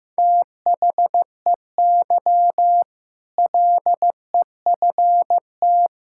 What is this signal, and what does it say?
Morse code, heard as a keyed tone, about 15 wpm, 700 Hz. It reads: THEY LEFT
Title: Sentences from Top 1000 Words - Review Part 6 - 15wpm